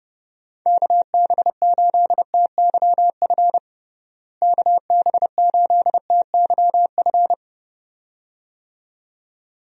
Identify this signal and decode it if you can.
Morse code, heard as a keyed tone, about 31 wpm, 700 Hz. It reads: KB8TYF KB8TYF
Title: USA Call Signs - 6 Characters - Repeated 2x - 30wpm